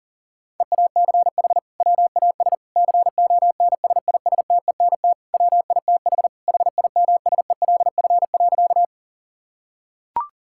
Morse code, heard as a keyed tone, about 40 wpm, 700 Hz. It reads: EACH WAS CONSISTENT WITH HIMSELF.
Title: Sentences from Top 1600 Words 40wpm